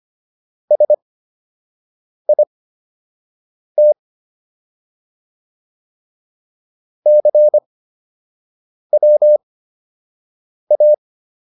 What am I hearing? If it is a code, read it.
Morse code, heard as a keyed tone, about 25 wpm, spaced as for 6 wpm, 600 Hz. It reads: SIT CWA